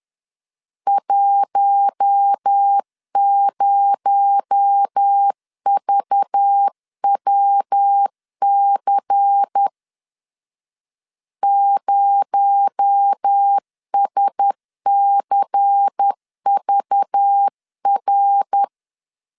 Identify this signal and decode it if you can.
Morse code, heard as a keyed tone, about 10 wpm, 800 Hz. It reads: 10VWC 0SCVR
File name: CW / DET / 1 6 1500C S B 10wpm 4137